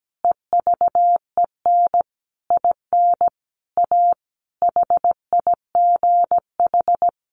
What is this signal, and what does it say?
Morse code, heard as a keyed tone, about 17 wpm, 700 Hz. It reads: EVEN IN A HIGH
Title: Sentences from Top 1500 Words 17wpm